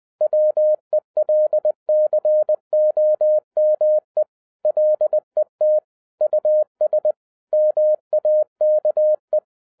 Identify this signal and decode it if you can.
Morse code, heard as a keyed tone, about 20 wpm, 610 Hz. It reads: WELCOME LET US MAKE